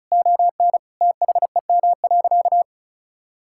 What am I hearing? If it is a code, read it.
Morse code, heard as a keyed tone, about 36 wpm, 700 Hz. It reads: ON THEM.
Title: Sentences from Top 1600 Words 35wpm